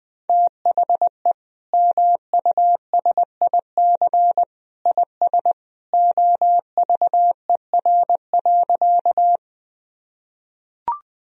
Morse code, heard as a keyed tone, about 20 wpm, 700 Hz. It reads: THE MUSIC IS OVER.
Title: Sentences from Top 1000 Words - Review Part 1 - 20wpm